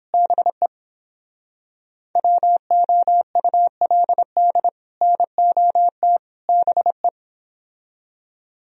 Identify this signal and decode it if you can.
Morse code, heard as a keyed tone, about 31 wpm, 700 Hz. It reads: BE WOULD NOT BE